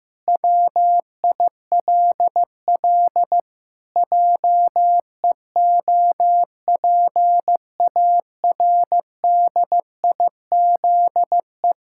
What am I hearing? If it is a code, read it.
Morse code, heard as a keyed tone, about 16 wpm, 700 Hz. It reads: WILL JEOPARDIZE
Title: Pangram Challenge v2 - 15wpm